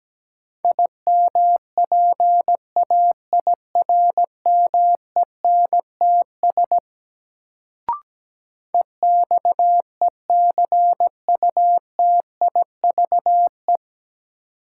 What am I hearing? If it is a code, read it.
Morse code, heard as a keyed tone, about 17 wpm, 700 Hz. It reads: IMPAIRMENTS EXECUTIVE
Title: Sets of 5 Words Encore - New Word Drill - 17wpm